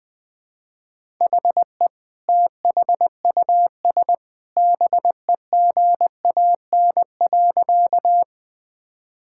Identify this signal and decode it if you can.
Morse code, heard as a keyed tone, about 21 wpm, 700 Hz. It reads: HE THUS BEGAN.
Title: Sentences from Top 1000 Words - Review Part 5 - 20wpm